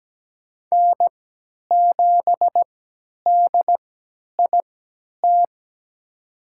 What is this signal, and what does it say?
Morse code, heard as a keyed tone, about 17 wpm, spaced as for 10 wpm, 700 Hz. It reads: N7DIT
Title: USA Call Signs - 5 Characters - 3x Character Spacing - 17wpm